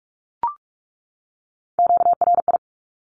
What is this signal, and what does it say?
Morse code, heard as a keyed tone, about 44 wpm, 700 Hz. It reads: QRI
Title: Q-Codes 45wpm